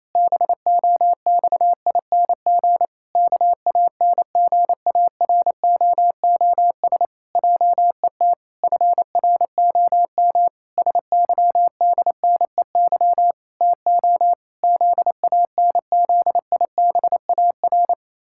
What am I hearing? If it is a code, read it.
Morse code, heard as a keyed tone, about 27 wpm, 700 Hz. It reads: BOXING KANGAROOS JET FROM SYDNEY TO ZANZIBAR
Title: Pangram Challenge v2 - 28wpm